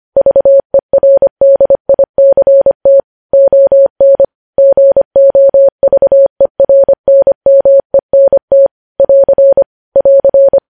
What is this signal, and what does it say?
Morse code, heard as a keyed tone, about 25 wpm, 550 Hz. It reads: VERDICT ON GOVERNMENT <AR> <AR>
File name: podcast / News headlines-1x-25x25